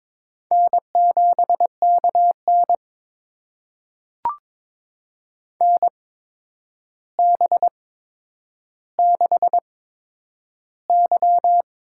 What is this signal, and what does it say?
Morse code, heard as a keyed tone, about 22 wpm, spaced as for 9 wpm, 700 Hz. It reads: N7KN NB6Y
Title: USA Call Signs - 4 Characters - 8x Character Spacing - 22wpm